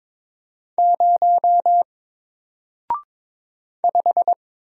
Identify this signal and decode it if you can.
Morse code, heard as a keyed tone, about 22 wpm, 700 Hz. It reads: 0 5
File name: Single-Letter-Number-22wpm